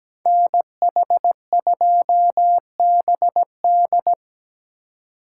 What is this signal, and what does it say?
Morse code, heard as a keyed tone, about 17 wpm, 700 Hz. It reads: NH2BD
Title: USA Call Signs - 5 Characters - 3x Character Spacing - 17wpm